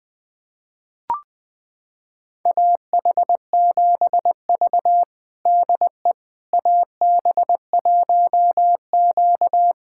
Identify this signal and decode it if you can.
Morse code, heard as a keyed tone, about 20 wpm, 700 Hz. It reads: AH7V DE AB1Q
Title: USA Callsign DE Callsign - 4 Characters - 20wpm